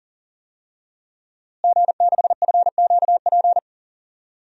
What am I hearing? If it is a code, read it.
Morse code, heard as a keyed tone, about 40 wpm, 700 Hz. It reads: G6FQP
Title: DXCC-100 Call Signs - 5 Characters - Repeated 3x - 40wpm